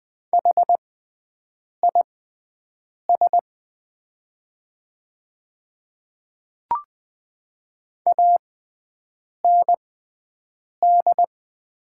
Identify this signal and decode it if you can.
Morse code, heard as a keyed tone, about 20 wpm, spaced as for 7 wpm, 700 Hz. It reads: HIS AND
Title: 6x Character Spacing - Top 200 Words - 20wpm